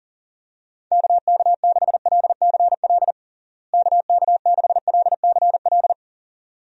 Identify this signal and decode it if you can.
Morse code, heard as a keyed tone, about 40 wpm, 700 Hz. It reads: KK6LCL KK6LCL
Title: USA Call Signs - 6 Characters - Repeated 2x - 40wpm